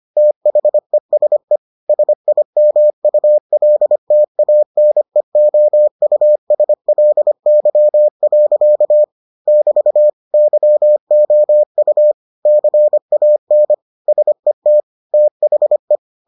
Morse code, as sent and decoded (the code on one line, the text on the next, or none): - .... ... / ... .. -- ..- .-.. - .- -. --- ..- ... .-.. -.-- .-.-.- / -...- / -.-- --- ..- / -.-. .- -. / ... - / - .... .
THESE SIMULTANEOUSLY. <BT> YOU CAN SET THE